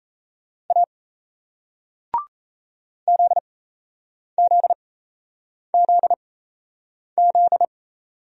Morse code, heard as a keyed tone, about 38 wpm, spaced as for 8 wpm, 700 Hz. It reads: A ZZZZ